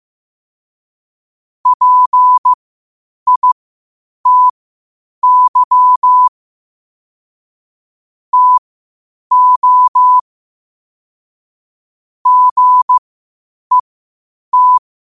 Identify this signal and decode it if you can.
Morse code, heard as a keyed tone, about 15 wpm, spaced as for 8 wpm, 1000 Hz. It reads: PITY TO GET
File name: qotdfiles / 8wpm